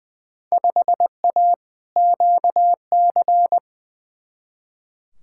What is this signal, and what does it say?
Morse code, heard as a keyed tone, about 20 wpm, 700 Hz. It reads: 5A QC
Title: Field Day Exchanges Encore - 20wpm